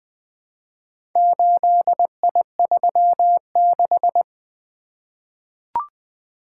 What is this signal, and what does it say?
Morse code, heard as a keyed tone, about 20 wpm, 700 Hz. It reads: 8I36
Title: Four Letter-Number Combinations - 20wpm